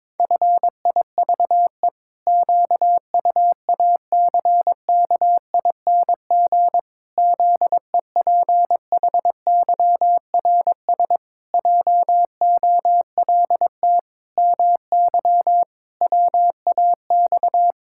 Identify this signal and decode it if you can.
Morse code, heard as a keyed tone, about 22 wpm, 700 Hz. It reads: FIVE QUACKING ZEPHYRS JOLT MY WAX
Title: Pangram Challenge v2 - 22wpm